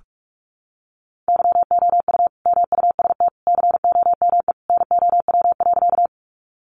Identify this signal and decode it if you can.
Morse code, heard as a keyed tone, about 44 wpm, 700 Hz. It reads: YOU MUST COME NOW.
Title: Sentences from Top 200 Words 45wpm